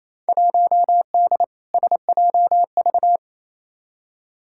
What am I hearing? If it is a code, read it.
Morse code, heard as a keyed tone, about 28 wpm, 700 Hz. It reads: 1D SJV
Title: Field Day Exchanges Encore - 28wpm